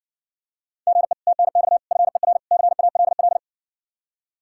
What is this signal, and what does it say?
Morse code, heard as a keyed tone, about 60 wpm, 700 Hz. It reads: GE TNX FER CALL